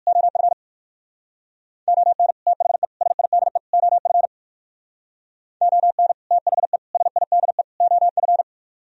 Morse code, heard as a keyed tone, about 60 wpm, 700 Hz. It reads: OF ON THE SIDE OF ON THE SIDE OF